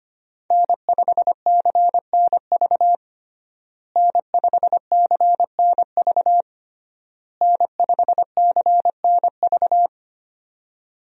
Morse code, heard as a keyed tone, about 25 wpm, 700 Hz. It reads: N5CNV N5CNV N5CNV
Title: USA Call Signs - 5 Characters - Repeated 3x - 25wpm